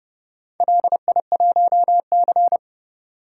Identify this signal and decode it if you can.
Morse code, heard as a keyed tone, about 30 wpm, 700 Hz. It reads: LI1C